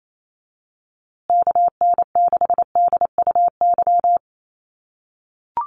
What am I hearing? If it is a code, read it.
Morse code, heard as a keyed tone, about 28 wpm, 700 Hz. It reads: KN6DUY
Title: USA Call Signs Repeated 3x 28wpm